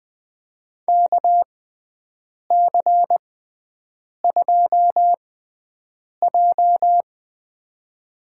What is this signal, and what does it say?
Morse code, heard as a keyed tone, about 20 wpm, spaced as for 7 wpm, 700 Hz. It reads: KC2J
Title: USA Call Signs - 4 Characters - 6x Character Spacing - 20wpm